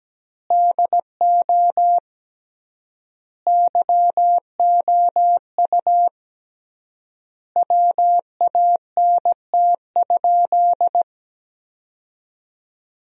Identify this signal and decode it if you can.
Morse code, heard as a keyed tone, about 17 wpm, 700 Hz. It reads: DO YOU WANT?
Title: 3x Word-Spacing - Sentences from Top 200 Words - 17wpm